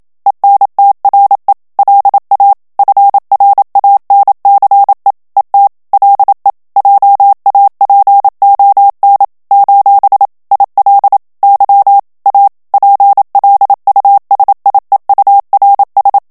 Morse code, heard as a keyed tone, about 27 wpm, 800 Hz. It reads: ENTRE LA FRANCE ET LE JAPON : IL Y A PLUSIEURS
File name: CW / FICHIERS-MP3 / FRANCAIS TEXTE CLAIR 1 VITESSE 26 MOTS MINUTES